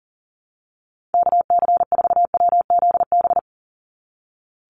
Morse code, heard as a keyed tone, about 38 wpm, 700 Hz. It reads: KC4WZB
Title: NA SOTA Activator Callsigns 40wpm